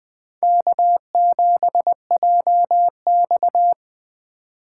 Morse code, heard as a keyed tone, about 19 wpm, 700 Hz. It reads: K7JX